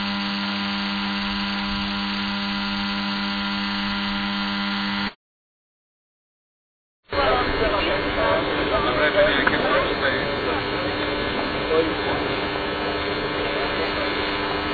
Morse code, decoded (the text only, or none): none